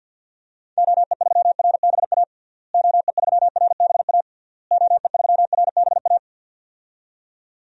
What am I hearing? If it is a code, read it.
Morse code, heard as a keyed tone, about 52 wpm, 700 Hz. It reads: OE3RBA OE3RBA OE3RBA
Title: DXCC-100 Call Signs - 6 Characters - Repeated 3x - 50wpm